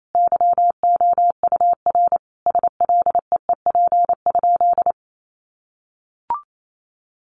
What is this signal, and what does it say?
Morse code, heard as a keyed tone, about 28 wpm, 700 Hz. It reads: YOUR SLEEP?
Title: Sentences from Top 500 Words 28wpm